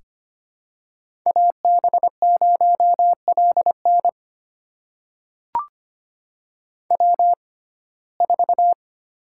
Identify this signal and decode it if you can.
Morse code, heard as a keyed tone, about 25 wpm, spaced as for 18 wpm, 700 Hz. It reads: AB0LN W4
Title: USA Call Signs - 5 Characters - 6x Character Spacing - 25wpm